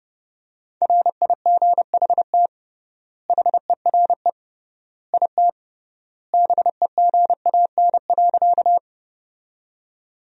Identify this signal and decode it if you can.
Morse code, heard as a keyed tone, about 30 wpm, 700 Hz. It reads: RIGHT HERE IT BEGAN.
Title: 3x Word-Spacing - Sentences from Top 300 Words - 30wpm